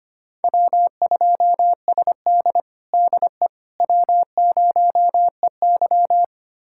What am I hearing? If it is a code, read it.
Morse code, heard as a keyed tone, about 25 wpm, 700 Hz. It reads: W2SD DE W0EY
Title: USA Callsign DE Callsign - 4 Characters - 25wpm